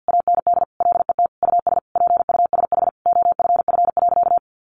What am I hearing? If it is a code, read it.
Morse code, heard as a keyed tone, about 50 wpm, 700 Hz. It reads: AND LET US PUSH OFF.
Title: Sentences from Top 1300 Words 50wpm